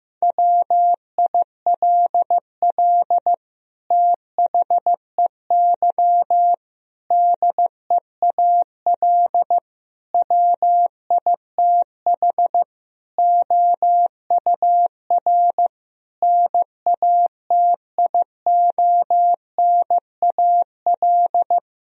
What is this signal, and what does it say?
Morse code, heard as a keyed tone, about 15 wpm, 700 Hz. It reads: WILL THEY DEAL WITH OUR NATIONAL
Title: Sentences from Top 1000 Words - Review Part 4 - 15wpm